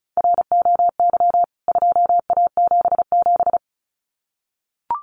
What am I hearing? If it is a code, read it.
Morse code, heard as a keyed tone, about 34 wpm, 700 Hz. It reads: ROY 2A77
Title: CWT Exchange 35wpm